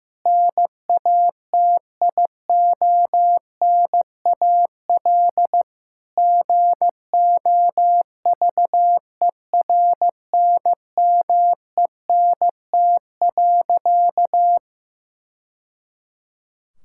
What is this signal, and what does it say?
Morse code, heard as a keyed tone, about 15 wpm, 700 Hz. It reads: NATIONAL GOVERNMENT.
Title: Sentences from Top 1000 Words - Review Part 5 - 15wpm